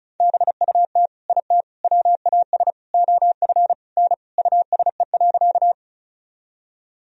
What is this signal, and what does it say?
Morse code, heard as a keyed tone, about 34 wpm, 700 Hz. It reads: BUT IT WAS OF N USE.
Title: Sentences from Top 100 Words and Abbreviations 35wpm